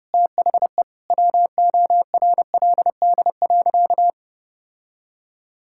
Morse code, heard as a keyed tone, about 30 wpm, 700 Hz. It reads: THE WORLD.
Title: Sentences from Top 1000 Words - Review Part 6 - 30wpm